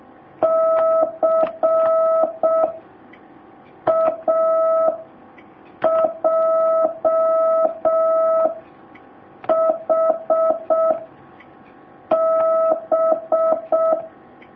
Morse code, heard as a keyed tone, about 6 wpm, 650 Hz. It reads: C A J H B
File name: jh8cbh / 2019cw